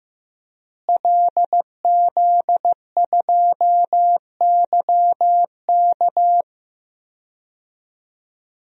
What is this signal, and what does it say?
Morse code, heard as a keyed tone, about 15 wpm, 700 Hz. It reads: LZ2YK
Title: DXCC-100 Call Signs - 5 Characters - Repeated 2x - 15wpm